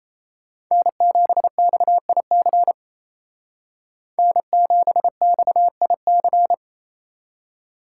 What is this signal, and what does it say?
Morse code, heard as a keyed tone, about 37 wpm, 700 Hz. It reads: N7XIC N7XIC